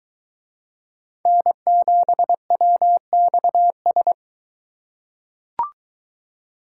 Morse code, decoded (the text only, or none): N7WXS